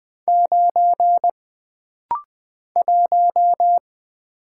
Morse code, heard as a keyed tone, about 20 wpm, 700 Hz. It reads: 9 1
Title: Single Letter-Number Alt - Rapid-Fire 20wpm